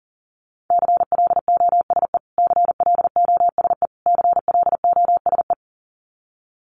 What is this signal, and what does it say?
Morse code, heard as a keyed tone, about 37 wpm, 700 Hz. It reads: CLOSE CLOSE CLOSE
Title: Top 300 Words Repeated 40wpm